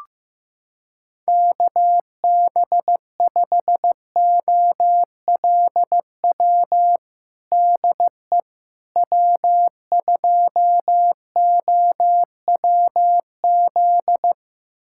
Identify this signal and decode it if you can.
Morse code, heard as a keyed tone, about 16 wpm, 700 Hz. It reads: KB5OLW DE W2OWZ
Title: USA Callsign DE Callsign 15wpm